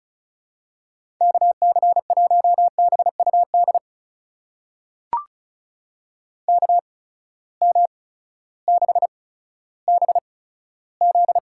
Morse code, heard as a keyed tone, about 36 wpm, spaced as for 16 wpm, 700 Hz. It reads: KC1BUD KM6BZ
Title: USA Call Signs - 6 Characters - 8x Character Spacing - 35wpm